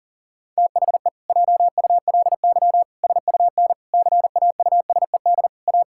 Morse code, heard as a keyed tone, about 40 wpm, 700 Hz. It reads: THE JULY SUN CAUSED A
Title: Pangram Challenge v2 - 40wpm